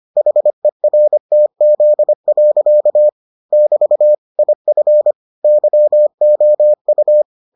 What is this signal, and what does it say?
Morse code, heard as a keyed tone, about 25 wpm, 600 Hz. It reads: HERTZ. <BT> IF YOU